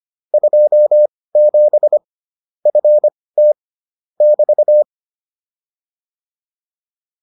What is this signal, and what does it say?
Morse code, heard as a keyed tone, about 25 wpm, spaced as for 18 wpm, 600 Hz. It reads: 27 FT <BT>
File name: QSO307 18